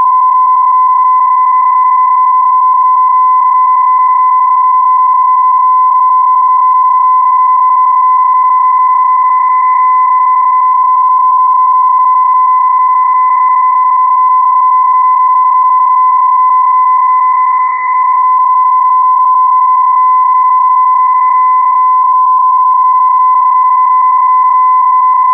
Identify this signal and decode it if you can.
PSK31, about 31.25 baud.